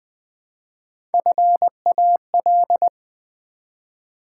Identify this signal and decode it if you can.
Morse code, heard as a keyed tone, about 20 wpm, 700 Hz. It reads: FAL
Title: Build A Word - 1 Letter At A Time - 4-Letter Words - 20wpm